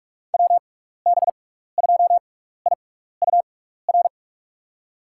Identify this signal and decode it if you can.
Morse code, heard as a keyed tone, about 43 wpm, spaced as for 15 wpm, 700 Hz. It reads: WB2IUR
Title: USA Call Signs - 6 Characters - 6x Character Spacing - 45wpm